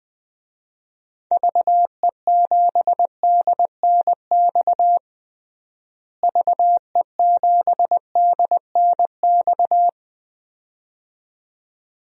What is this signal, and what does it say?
Morse code, heard as a keyed tone, about 20 wpm, 700 Hz. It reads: VE7DNX VE7DNX
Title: DXCC-100 Call Signs - 6 Characters - Repeated 2x - 20wpm